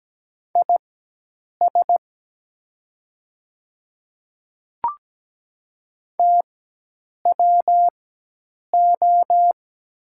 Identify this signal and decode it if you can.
Morse code, heard as a keyed tone, about 17 wpm, spaced as for 8 wpm, 700 Hz. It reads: IS TWO
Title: 4x Character Spacing - Top 100 Words - 17wpm